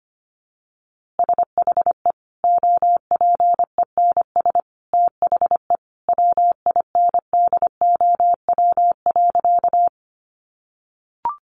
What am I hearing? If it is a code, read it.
Morse code, heard as a keyed tone, about 25 wpm, 700 Hz. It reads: SHE OPENS THE WINDOW.
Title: Easy Phrases 25wpm